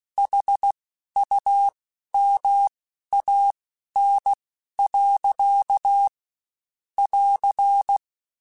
Morse code, heard as a keyed tone, about 16 wpm, spaced as for 11 wpm, 800 Hz. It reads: HUMAN. <AR>
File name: Spaced / Chuckles Spaced-mp3s / HumanErr sp @15